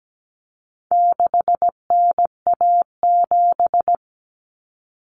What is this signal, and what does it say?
Morse code, heard as a keyed tone, about 17 wpm, 700 Hz. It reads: 6NA7
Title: Common Cut Numbers - 4-Digit 17wpm